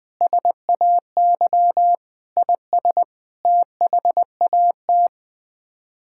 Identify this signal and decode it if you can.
Morse code, heard as a keyed tone, about 20 wpm, 700 Hz. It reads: SAY IS THAT
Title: Sets of 6 Words - 20wpm